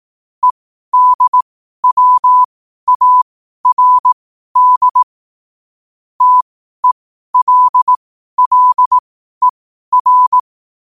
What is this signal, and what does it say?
Morse code, heard as a keyed tone, about 18 wpm, spaced as for 12 wpm, 1000 Hz. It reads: EDWARD TELLER
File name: qotdfiles / 12wpm